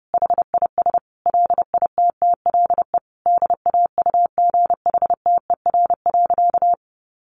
Morse code, heard as a keyed tone, about 28 wpm, 700 Hz. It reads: HIS LITTLE DAUGHTER.
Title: Sentences from Top 1500 Words 30wpm